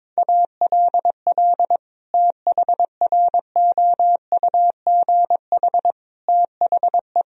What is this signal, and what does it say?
Morse code, heard as a keyed tone, about 22 wpm, 700 Hz. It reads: ALL THROUGH THE
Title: Sentences from Top 1000 Words - Review Part 2 - 22wpm